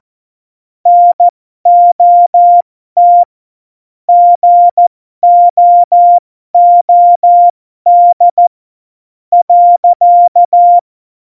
Morse code, heard as a keyed tone, about 14 wpm, spaced as for 12 wpm, 700 Hz. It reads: NOT GOOD .
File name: Morsekurs / Witze